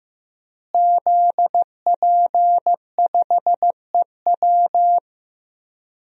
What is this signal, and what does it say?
Morse code, heard as a keyed tone, about 15 wpm, 700 Hz. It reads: ZP5EW
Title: DXCC-100 Call Signs - 5 Characters - Repeated 2x - 15wpm